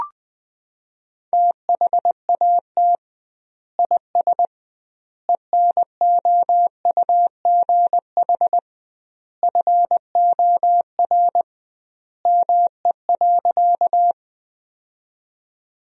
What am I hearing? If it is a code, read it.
Morse code, heard as a keyed tone, about 20 wpm, 700 Hz. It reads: THAT IS ENOUGH FOR ME.
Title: 2x Word-Spacing - Sentences from Top 400 Words - 20wpm